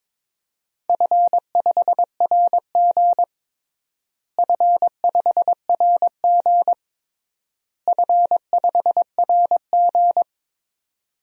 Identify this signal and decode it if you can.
Morse code, heard as a keyed tone, about 22 wpm, 700 Hz. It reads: F5RG F5RG F5RG
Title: DXCC-100 Call Signs - 4 Characters - Repeated 3x - 22wpm